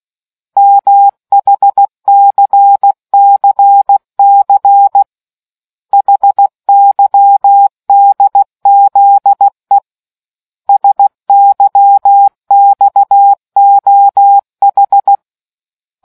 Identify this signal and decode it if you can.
Morse code, heard as a keyed tone, about 16 wpm, 800 Hz. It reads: MHCCC HYDZE SYXOH